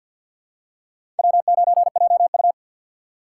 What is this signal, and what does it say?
Morse code, heard as a keyed tone, about 51 wpm, 700 Hz. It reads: W9JU
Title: USA Call Signs - 2x Character Spacing - 50wpm